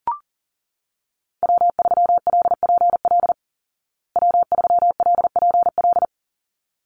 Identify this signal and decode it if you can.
Morse code, heard as a keyed tone, about 40 wpm, 700 Hz. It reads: W3LPL W3LPL
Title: Call Signs Repeated 3x - 40wpm